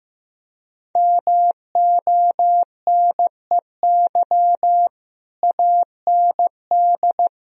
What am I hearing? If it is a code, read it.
Morse code, heard as a keyed tone, about 15 wpm, 700 Hz. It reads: MONEY AND
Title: Sentences from Top 1000 Words - Review Part 3 - 15wpm